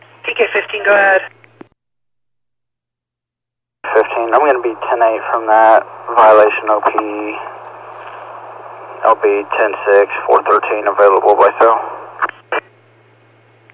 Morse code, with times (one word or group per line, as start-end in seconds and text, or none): none